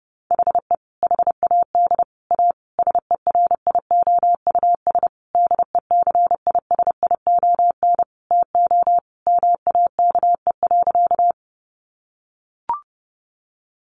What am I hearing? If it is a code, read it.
Morse code, heard as a keyed tone, about 30 wpm, 700 Hz. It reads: HE HAD A SERIOUS DECISION TO MAKE.
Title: Sentences from Top 1100 Words 30wpm